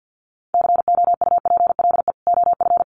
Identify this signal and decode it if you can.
Morse code, heard as a keyed tone, about 50 wpm, 700 Hz. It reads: COUPLE OF
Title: Sets of 2 Words 50wpm